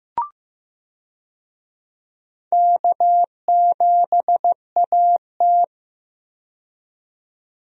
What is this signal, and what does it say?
Morse code, heard as a keyed tone, about 15 wpm, 700 Hz. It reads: K7AT
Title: USA Call Signs - 4 Characters - 15wpm